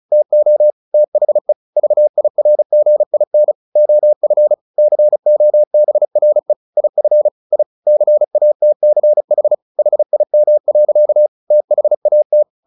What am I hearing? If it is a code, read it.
Morse code, heard as a keyed tone, about 34 wpm, 600 Hz. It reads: TO THE VIRGIN OF COBRE IF I CATCH HIM. THAT
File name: the old man and the sea / the old man and the sea 1 35wpm